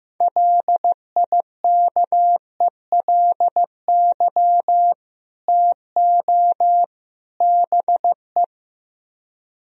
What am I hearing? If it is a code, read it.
Morse code, heard as a keyed tone, about 15 wpm, 700 Hz. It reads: LIKELY TO BE
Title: Speed-Racing - Sets of 3 Words 15wpm